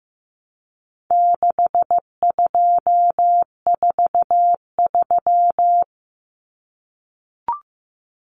Morse code, heard as a keyed tone, about 15 wpm, 700 Hz. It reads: 6243
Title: Numbers - 4-Digit 15wpm